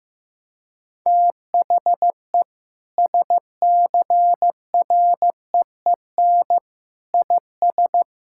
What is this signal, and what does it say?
Morse code, heard as a keyed tone, about 15 wpm, 700 Hz. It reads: THE SCREEN IS
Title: Sentences from Top 1000 Words - Review Part 6 - 15wpm